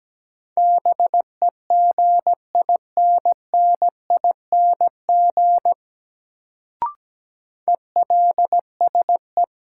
Morse code, heard as a keyed tone, about 17 wpm, 700 Hz. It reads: BEGINNING ELSE
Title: Sentences from Top 800 Words - New Word Drill - 17wpm